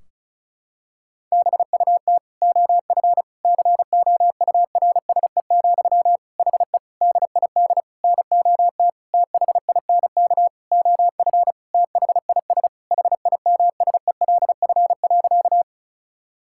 Morse code, heard as a keyed tone, about 34 wpm, 700 Hz. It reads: BUT OF COURSE, HE DID NOT THINK OF THIS HIMSELF.
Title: Sentences from Top 1600 Words 35wpm